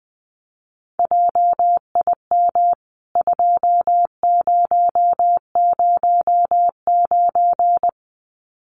Morse code, heard as a keyed tone, about 20 wpm, 700 Hz. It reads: JIM 2009
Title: CWT Exchange 20wpm